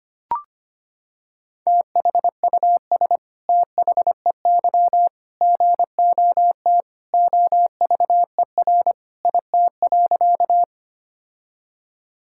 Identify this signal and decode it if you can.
Morse code, heard as a keyed tone, about 25 wpm, 700 Hz. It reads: THUS THEY GOT OVER IT.
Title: Sentences from Top 1000 Words - Review Part 5 - 25wpm